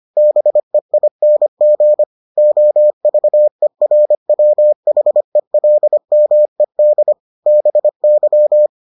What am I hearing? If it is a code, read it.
Morse code, heard as a keyed tone, about 25 wpm, 600 Hz. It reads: BEING OVERWHELMED BY